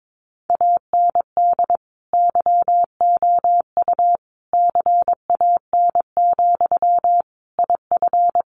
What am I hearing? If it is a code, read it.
Morse code, heard as a keyed tone, about 22 wpm, 700 Hz. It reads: AND YOU CAN, IF
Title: Sentences from Top 1500 Words 22wpm